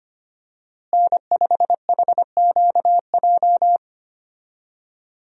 Morse code, heard as a keyed tone, about 25 wpm, 700 Hz. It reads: N5HQJ
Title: USA Call Signs - 5 Characters - Repeated 3x - 25wpm